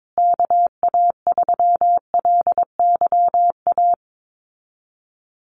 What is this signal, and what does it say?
Morse code, heard as a keyed tone, about 22 wpm, 700 Hz. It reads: KA3LYA
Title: USA Call Signs Repeated 3x 22wpm